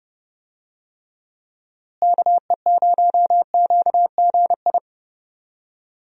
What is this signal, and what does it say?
Morse code, heard as a keyed tone, about 30 wpm, 700 Hz. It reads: KE0QGI
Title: USA Call Signs - 6 Characters - Repeated 2x - 30wpm